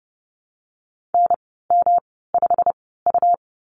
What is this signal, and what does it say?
Morse code, heard as a keyed tone, about 30 wpm, spaced as for 17 wpm, 700 Hz. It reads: NM5U
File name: Lesson 19g - Letter w - Call Signs 30wpm